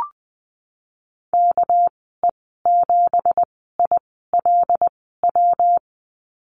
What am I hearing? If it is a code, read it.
Morse code, heard as a keyed tone, about 20 wpm, spaced as for 15 wpm, 700 Hz. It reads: KE7ILW
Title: 200 Lesson 32g - Letter k - Call Signs 20wpm